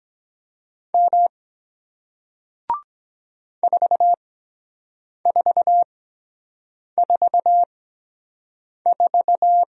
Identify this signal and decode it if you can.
Morse code, heard as a keyed tone, about 22 wpm, spaced as for 7 wpm, 700 Hz. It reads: M 4444